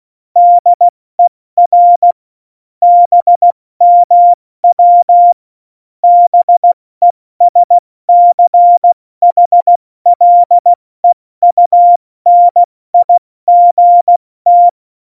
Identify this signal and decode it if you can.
Morse code, heard as a keyed tone, about 16 wpm, 700 Hz. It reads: DER BMW BESCHLEUNIGT